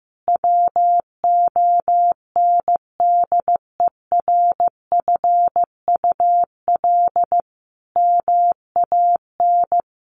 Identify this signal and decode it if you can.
Morse code, heard as a keyed tone, about 15 wpm, 700 Hz. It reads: WONDERFUL MAN
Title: Sentences from Top 1100 Words 15wpm